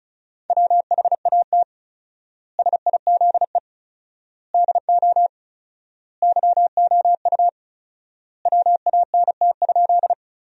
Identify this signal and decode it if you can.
Morse code, heard as a keyed tone, about 34 wpm, 700 Hz. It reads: WHAT SIZE DO YOU WANT?
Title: 4x Word-Spacing - Sentences from Top 400 Words - 35wpm